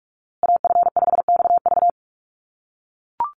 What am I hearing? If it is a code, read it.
Morse code, heard as a keyed tone, about 44 wpm, 700 Hz. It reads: AF5XV
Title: USA Call Signs Repeated 2x 45wpm